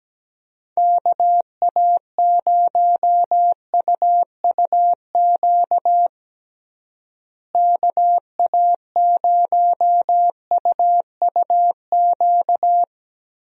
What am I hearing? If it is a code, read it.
Morse code, heard as a keyed tone, about 17 wpm, 700 Hz. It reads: KA0UUQ KA0UUQ
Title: USA Call Signs - 6 Characters - Repeated 3x - 17wpm